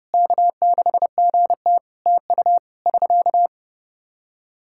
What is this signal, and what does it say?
Morse code, heard as a keyed tone, about 30 wpm, 700 Hz. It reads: K6GT TU <SK>
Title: Practice QSO - Exchange 4 - Ending - 30wpm